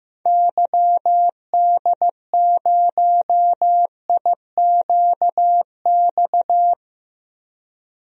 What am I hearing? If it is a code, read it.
Morse code, heard as a keyed tone, about 15 wpm, 700 Hz. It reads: YD0IQX
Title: DXCC-100 Call Signs - 6 Characters - 15wpm